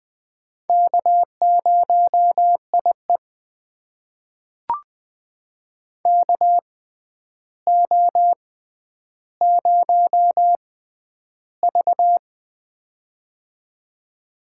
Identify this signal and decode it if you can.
Morse code, heard as a keyed tone, about 20 wpm, spaced as for 10 wpm, 700 Hz. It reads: K0IE KO0V
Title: USA Call Signs - 4 Characters - 6x Character Spacing - 20wpm